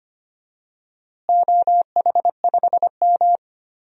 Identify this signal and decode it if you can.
Morse code, heard as a keyed tone, about 25 wpm, 700 Hz. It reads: OH5M
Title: DXCC-100 Call Signs - 4 Characters - Repeated 3x - 25wpm